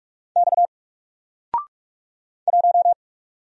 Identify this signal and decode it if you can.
Morse code, heard as a keyed tone, about 44 wpm, 700 Hz. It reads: X 1
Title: Single Letter-Number Alt 45wpm